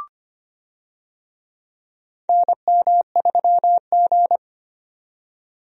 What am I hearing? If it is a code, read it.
Morse code, heard as a keyed tone, about 25 wpm, 700 Hz. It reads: NM3G
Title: USA Call Signs - 4 Characters - 25wpm